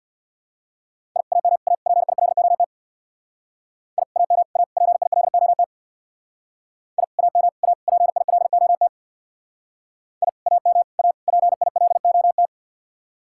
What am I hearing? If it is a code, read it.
Morse code, heard as a keyed tone, about 67 wpm, 700 Hz. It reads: I AM A PILOT I AM A PILOT I AM A PILOT I AM A PILOT